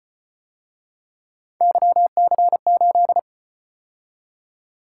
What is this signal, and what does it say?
Morse code, heard as a keyed tone, about 33 wpm, 700 Hz. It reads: YC8